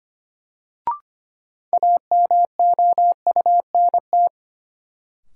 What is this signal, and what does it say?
Morse code, heard as a keyed tone, about 25 wpm, 700 Hz. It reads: AMOUNT
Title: Sets of 3 Words - New Word Drill - 25wpm